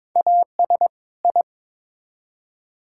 Morse code, heard as a keyed tone, about 22 wpm, 700 Hz. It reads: AS I
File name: Build A Phrase - 2-Word Phrases - 22wpm